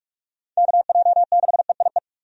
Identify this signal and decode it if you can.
Morse code, heard as a keyed tone, about 46 wpm, 700 Hz. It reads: KJ6EIE
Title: USA Call Signs - 6 Characters - 45wpm